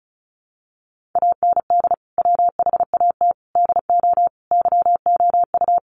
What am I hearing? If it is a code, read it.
Morse code, heard as a keyed tone, about 34 wpm, 700 Hz. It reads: AND WHAT DO YOU
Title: Sentences from Top 100 Words 35wpm